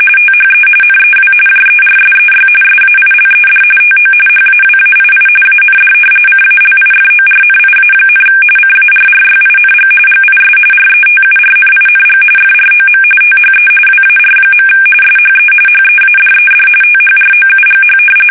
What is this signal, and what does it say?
RTTY, about 45.45 baud. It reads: THE NEW JERSEY SAILED TO HAWAII AND CALIFORNIA, CARRYING GIS RETURNING FROM THE WAR. SHE HEADED THROUGH
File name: fsk850 / battleshipnj